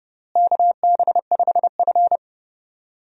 Morse code, heard as a keyed tone, about 29 wpm, 700 Hz. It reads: KB5F
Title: USA Call Signs - 4 Characters - Repeated 3x - 30wpm